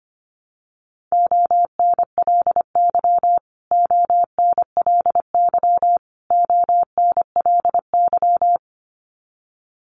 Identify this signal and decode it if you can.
Morse code, heard as a keyed tone, about 25 wpm, 700 Hz. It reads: ONLY ONLY ONLY
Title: Top 200 Words Repeated 25wpm